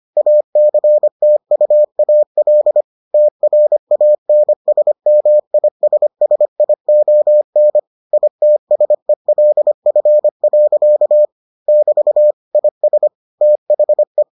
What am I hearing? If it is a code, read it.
Morse code, heard as a keyed tone, about 25 wpm, 600 Hz. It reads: ACTUAL TRANSMISSION ITSELF. <BT> IS THE